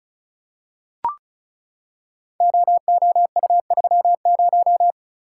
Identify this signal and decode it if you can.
Morse code, heard as a keyed tone, about 36 wpm, 700 Hz. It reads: OOU30
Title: Code Groups - 5 Letter:Numbers - 35wpm